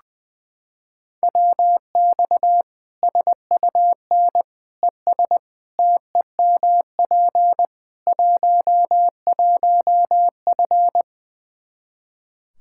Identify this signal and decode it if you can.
Morse code, heard as a keyed tone, about 21 wpm, 700 Hz. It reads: WX SUN ES TEMP 11F
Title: Practice QSO - Exchange 2 - Info - 20wpm